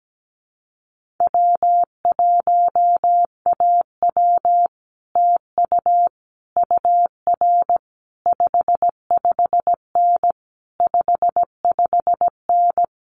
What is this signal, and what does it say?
Morse code, heard as a keyed tone, about 17 wpm, 700 Hz. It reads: W1AW TU UR 55N 55N
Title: Common QSO Phrases Encore 17wpm